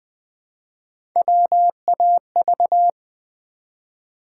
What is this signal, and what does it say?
Morse code, heard as a keyed tone, about 20 wpm, 700 Hz. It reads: WAV
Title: Build A Word - 1 Letter At A Time - 4-Letter Words - 20wpm